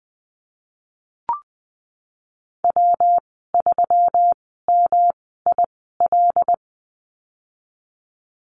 Morse code, heard as a keyed tone, about 19 wpm, spaced as for 14 wpm, 700 Hz. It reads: W3MIL
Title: 122 Lesson 20g - Number 3 - Call Signs 20wpm